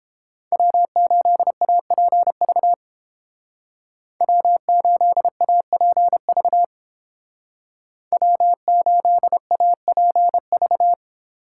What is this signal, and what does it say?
Morse code, heard as a keyed tone, about 29 wpm, 700 Hz. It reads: W8APV W8APV W8APV